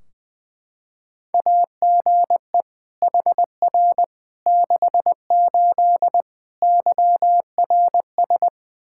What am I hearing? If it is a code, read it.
Morse code, heard as a keyed tone, about 20 wpm, 700 Hz. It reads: AGE HR 68 YRS
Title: K1USN SST QSO Practice 20wpm